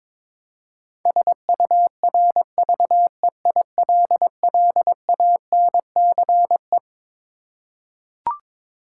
Morse code, heard as a keyed tone, about 22 wpm, 700 Hz. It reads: SURVEILLANCE
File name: Twelve Letter Words - 22wpm